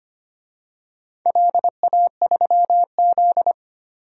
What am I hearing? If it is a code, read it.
Morse code, heard as a keyed tone, about 25 wpm, 700 Hz. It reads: LA3Z
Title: DXCC-100 Call Signs - 4 Characters - Repeated 3x - 25wpm